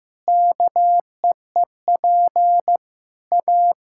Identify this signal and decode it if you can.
Morse code, heard as a keyed tone, about 15 wpm, 700 Hz. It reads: KEEP A
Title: Sentences from Top 1000 Words - Review Part 6 - 15wpm